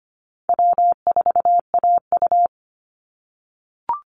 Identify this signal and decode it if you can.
Morse code, heard as a keyed tone, about 25 wpm, 700 Hz. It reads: W4AU
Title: Easy Call Signs - 25wpm